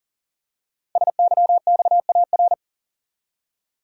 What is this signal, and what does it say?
Morse code, heard as a keyed tone, about 39 wpm, 700 Hz. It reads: SYXAR
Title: Code Groups - 5 Letters - 40wpm